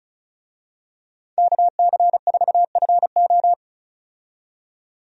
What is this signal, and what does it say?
Morse code, heard as a keyed tone, about 34 wpm, 700 Hz. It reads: KC4FO
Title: USA Call Signs - 5 Characters - 35wpm